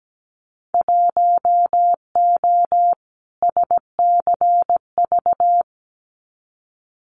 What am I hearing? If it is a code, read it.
Morse code, heard as a keyed tone, about 17 wpm, 690 Hz. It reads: 1O SCV
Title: Winter Field Day Exchanges 17wpm